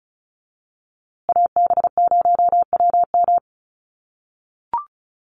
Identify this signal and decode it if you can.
Morse code, heard as a keyed tone, about 34 wpm, 700 Hz. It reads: AB0WM
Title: USA Call Signs Repeated 2x 35wpm